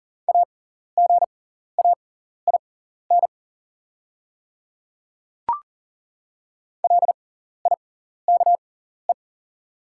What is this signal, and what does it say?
Morse code, heard as a keyed tone, about 40 wpm, spaced as for 14 wpm, 690 Hz. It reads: AGAIN LIKE
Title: 6x Character Spacing - Top 200 Words - 40wpm